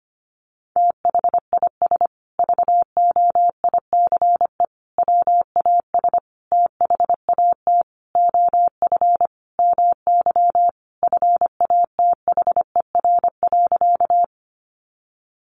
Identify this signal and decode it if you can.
Morse code, heard as a keyed tone, about 25 wpm, 700 Hz. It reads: THIS VOICE WAS THAT OF MY FATHER.
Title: Sentences from Top 500 Words 25wpm